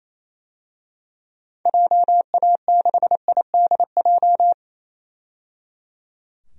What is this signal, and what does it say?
Morse code, heard as a keyed tone, about 28 wpm, 700 Hz. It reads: JA6IDJ
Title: DXCC-100 Call Signs - 6 Characters - 28wpm